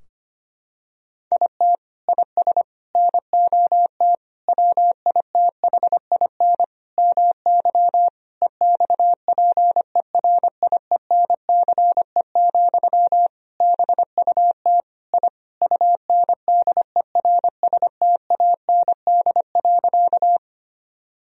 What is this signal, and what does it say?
Morse code, heard as a keyed tone, about 25 wpm, 700 Hz. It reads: IT IS NOT WITHIN MY EXPERIENCE, BUT I UNDERSTAND.
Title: Sentences from Top 1000 Words - Review Part 4 - 25wpm